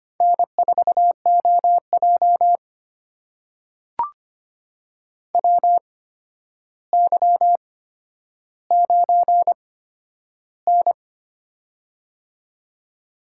Morse code, heard as a keyed tone, about 25 wpm, spaced as for 11 wpm, 700 Hz. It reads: N4OJ WY9N
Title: USA Call Signs - 4 Characters - 8x Character Spacing - 25wpm